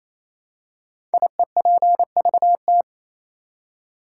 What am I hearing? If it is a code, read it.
Morse code, heard as a keyed tone, about 27 wpm, 700 Hz. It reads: IEPVT
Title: Code Groups - 5 Letters - 28wpm